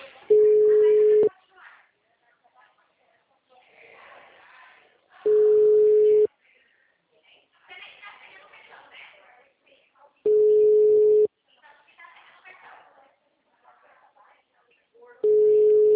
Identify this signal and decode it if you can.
Ringback tone (Europe, 425 Hz), with speech.
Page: Leme 2